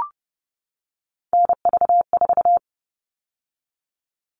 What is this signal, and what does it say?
Morse code, heard as a keyed tone, about 30 wpm, 700 Hz. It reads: NV4
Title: All Cut Numbers - 3-Digit 30wpm